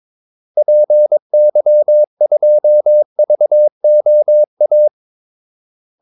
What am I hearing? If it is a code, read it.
Morse code, heard as a keyed tone, about 22 wpm, 600 Hz. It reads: PY2VOA
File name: Callsigns-22wpm